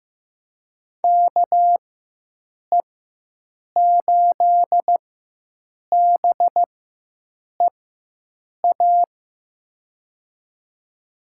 Morse code, heard as a keyed tone, about 15 wpm, spaced as for 7 wpm, 700 Hz. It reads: KE8BEA
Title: USA Call Signs - 6 Characters - 4x Character Spacing - 15wpm